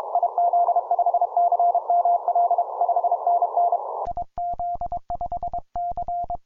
Morse code, heard as a keyed tone, about 24 wpm, 700 Hz. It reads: IZ5CML 5NN IZ5C